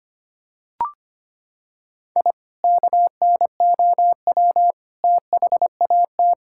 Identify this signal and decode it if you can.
Morse code, heard as a keyed tone, about 25 wpm, 700 Hz. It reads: I KNOW THAT
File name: practice / 25wpm / Sentences with 2-Word Phrases - 25wpm